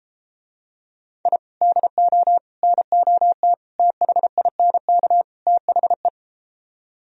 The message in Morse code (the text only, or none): I DO NOT THINK THE